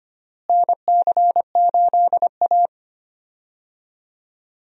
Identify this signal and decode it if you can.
Morse code, heard as a keyed tone, about 25 wpm, 700 Hz. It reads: NC8A
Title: USA Call Signs - 4 Characters - Repeated 3x - 25wpm